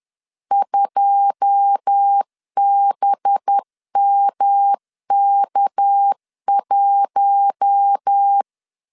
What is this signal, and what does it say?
Morse code, heard as a keyed tone, about 10 wpm, 800 Hz. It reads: 2BMK1
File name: LEZIONI / 9 2 10wpm 2736